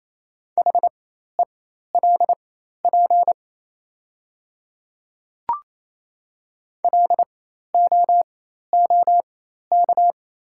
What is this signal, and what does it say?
Morse code, heard as a keyed tone, about 28 wpm, spaced as for 13 wpm, 700 Hz. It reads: HELP LOOK
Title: 4x Character Spacing - Top 300 Words - 28wpm